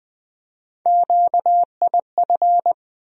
Morse code, heard as a keyed tone, about 21 wpm, 700 Hz. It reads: QIF
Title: USA Call Sign Postfixes - 3 Letters 20wpm